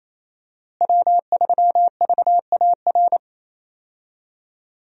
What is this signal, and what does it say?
Morse code, heard as a keyed tone, about 28 wpm, 700 Hz. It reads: W3VAR
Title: USA Call Signs - 5 Characters - Repeated 3x - 28wpm